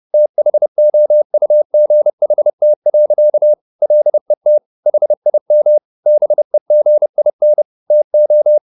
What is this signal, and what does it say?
Morse code, heard as a keyed tone, about 30 wpm, 600 Hz. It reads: THOUGHT. LET HIM BEGIN TO